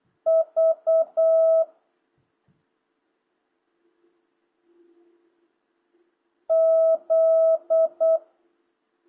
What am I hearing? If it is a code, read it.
Morse code, heard as a keyed tone, about 8 wpm, 650 Hz. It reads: V Z